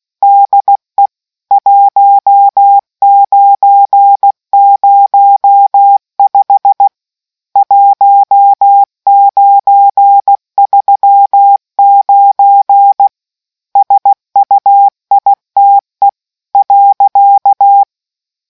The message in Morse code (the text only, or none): DE 1905 1939 SUITE.